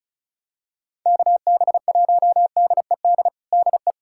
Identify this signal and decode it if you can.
Morse code, heard as a keyed tone, about 36 wpm, 700 Hz. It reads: KB1DED DE
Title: USA Callsign DE Callsign - 6 Characters - 35wpm